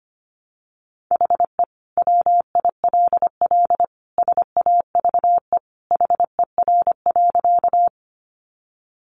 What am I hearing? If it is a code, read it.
Morse code, heard as a keyed tone, about 25 wpm, 700 Hz. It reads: HE WILL SAVE HER.
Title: Sentences from Top 1100 Words 25wpm